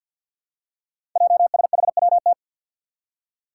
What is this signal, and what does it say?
Morse code, heard as a keyed tone, about 53 wpm, 700 Hz. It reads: JSHWT
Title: Five Challenging Letter-Number Combinations 50wpm